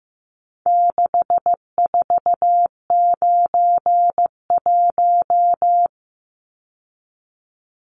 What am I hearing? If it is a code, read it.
Morse code, heard as a keyed tone, about 15 wpm, 700 Hz. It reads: 6491
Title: Numbers - 4-Digit 15wpm